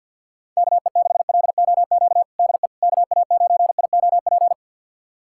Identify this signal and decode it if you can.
Morse code, heard as a keyed tone, about 47 wpm, 690 Hz. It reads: KE6LOQ DE KA9IOP
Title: USA Callsign DE Callsign 50wpm